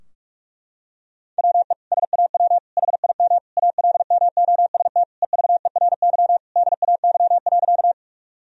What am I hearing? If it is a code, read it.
Morse code, heard as a keyed tone, about 45 wpm, 700 Hz. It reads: WE SAW HIM ALMOST EVERY DAY.
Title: Sentences from Top 1000 Words - Review Part 4 - 45wpm